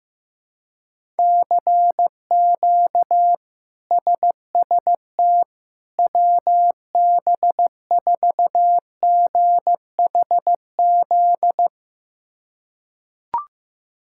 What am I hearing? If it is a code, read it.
Morse code, heard as a keyed tone, about 15 wpm, 700 Hz. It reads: CQ SST WB4GHZ
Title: K1USN SST QSO Practice - 15wpm